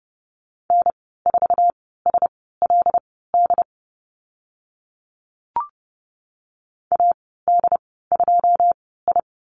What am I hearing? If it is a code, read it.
Morse code, heard as a keyed tone, about 30 wpm, spaced as for 17 wpm, 700 Hz. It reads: N4SLD AD2I
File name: Lesson 14g - Number 2 - Call Signs 30wpm